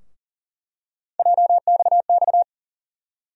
Morse code, heard as a keyed tone, about 40 wpm, 700 Hz. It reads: JXX
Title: USA Call Sign Postfixes - 3 Letters 40wpm